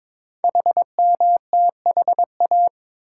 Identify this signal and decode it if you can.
Morse code, heard as a keyed tone, about 22 wpm, 700 Hz. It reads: HMTHA
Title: Code Groups - 5 Letters - 22wpm